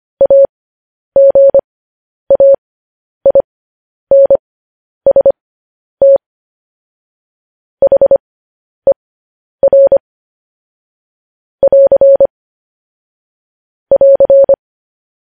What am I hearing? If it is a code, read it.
Morse code, heard as a keyed tone, about 25 wpm, spaced as for 10 wpm, 550 Hz. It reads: AGAINST HER <AR> <AR>
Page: news-headlines-in-morse-code-at-10-wpm